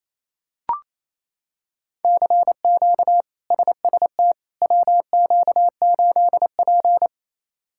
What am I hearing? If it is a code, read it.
Morse code, heard as a keyed tone, about 28 wpm, 700 Hz. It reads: CQ SST WQ8P
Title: K1USN SST QSO Practice - 28wpm